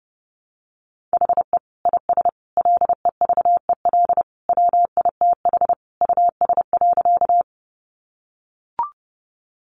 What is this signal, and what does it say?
Morse code, heard as a keyed tone, about 30 wpm, 700 Hz. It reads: HE IS LEVEL WITH US.
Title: Sentences from Top 700 Words 30wpm